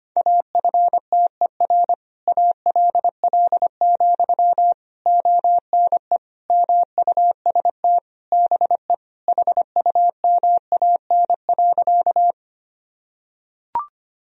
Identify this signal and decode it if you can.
Morse code, heard as a keyed tone, about 25 wpm, 700 Hz. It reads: AFTER ALL, ONE MUST BE HUMAN.
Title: Sentences from Top 1000 Words - Review Part 3 - 25wpm